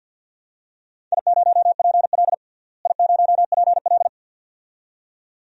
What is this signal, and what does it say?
Morse code, heard as a keyed tone, about 46 wpm, 700 Hz. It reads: I0PL I0PL
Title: DXCC-100 Call Signs - 4 Characters - Repeated 2x - 50wpm